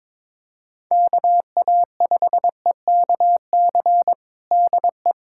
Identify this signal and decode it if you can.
Morse code, heard as a keyed tone, about 22 wpm, 700 Hz. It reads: KA5EKC DE
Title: USA Callsign DE Callsign - 6 Characters - 22wpm